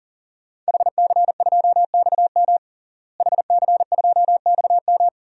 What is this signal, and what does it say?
Morse code, heard as a keyed tone, about 37 wpm, 700 Hz. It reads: HC2XM HC2XM
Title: DXCC-100 Call Signs - 5 Characters - Repeated 2x - 40wpm